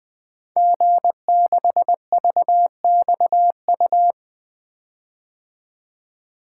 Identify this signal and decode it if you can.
Morse code, heard as a keyed tone, about 20 wpm, 700 Hz. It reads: G6VXU
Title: Code Groups - 5 Letter:Numbers - 20wpm